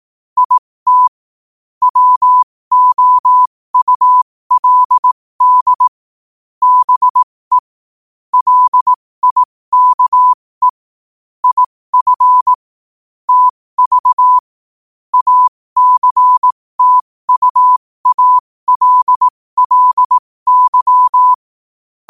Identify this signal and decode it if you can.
Morse code, heard as a keyed tone, about 18 wpm, spaced as for 15 wpm, 1000 Hz. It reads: IT WOULD BE LIKE IF TV ACTUALLY
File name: qotdfiles / 15wpm